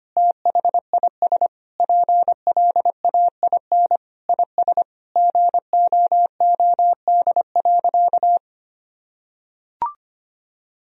Morse code, heard as a keyed tone, about 25 wpm, 700 Hz. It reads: THIS PLAIN IS GOOD.
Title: Sentences from Top 1000 Words - Review Part 4 - 25wpm